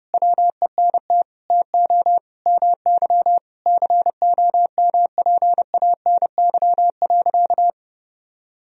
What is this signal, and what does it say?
Morse code, heard as a keyed tone, about 30 wpm, 700 Hz. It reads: WENT TO MY COMPANY.
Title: Sentences from Top 1000 Words - Review Part 3 - 30wpm